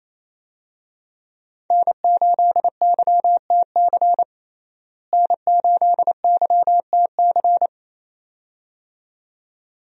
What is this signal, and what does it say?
Morse code, heard as a keyed tone, about 28 wpm, 700 Hz. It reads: N8YTC N8YTC
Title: USA Call Signs - 5 Characters - Repeated 2x - 28wpm